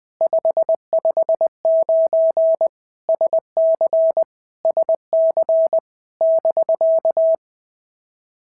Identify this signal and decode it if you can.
Morse code, heard as a keyed tone, about 20 wpm, 650 Hz. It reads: 559 SC SC <BK>
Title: POTA QSO Practice - 20wpm